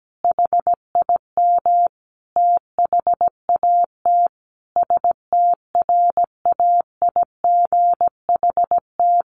Morse code, heard as a keyed tone, about 17 wpm, 700 Hz. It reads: HIM THAT STRAIGHT
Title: Sentences from Top 1500 Words 17wpm